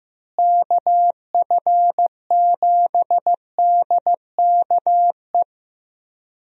Morse code, heard as a keyed tone, about 14 wpm, 700 Hz. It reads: KF7DKE
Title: USA Call Signs - 6 Characters - Repeated 3x - 15wpm